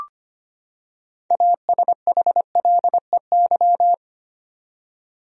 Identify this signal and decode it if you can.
Morse code, heard as a keyed tone, about 25 wpm, 700 Hz. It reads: ASHLEY
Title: USA Common Names - 25wpm